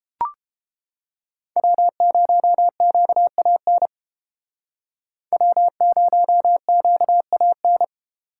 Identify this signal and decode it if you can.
Morse code, heard as a keyed tone, about 31 wpm, 700 Hz. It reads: W0QAN W0QAN